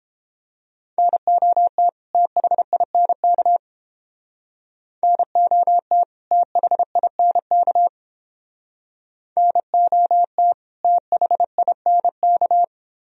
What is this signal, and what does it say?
Morse code, heard as a keyed tone, about 31 wpm, 700 Hz. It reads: NOT THINK NOT THINK NOT THINK